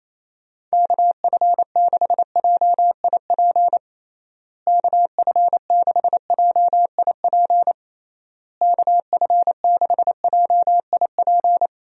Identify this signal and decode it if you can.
Morse code, heard as a keyed tone, about 28 wpm, 700 Hz. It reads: KF6JIP KF6JIP KF6JIP
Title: USA Call Signs - 6 Characters - Repeated 3x - 28wpm